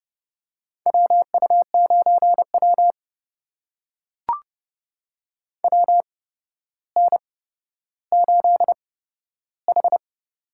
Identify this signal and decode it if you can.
Morse code, heard as a keyed tone, about 31 wpm, spaced as for 13 wpm, 700 Hz. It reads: WU9W WN8H
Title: USA Call Signs - 4 Characters - 8x Character Spacing - 30wpm